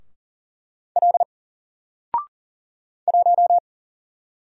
Morse code, heard as a keyed tone, about 40 wpm, 700 Hz. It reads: L 1
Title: Single Character (Letter:Number:Punctuation:Prosign) Alt 40wpm